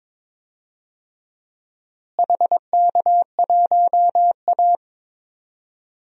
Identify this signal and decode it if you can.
Morse code, heard as a keyed tone, about 22 wpm, 700 Hz. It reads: HK1A
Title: DXCC-100 Call Signs - 4 Characters - Repeated 2x - 22wpm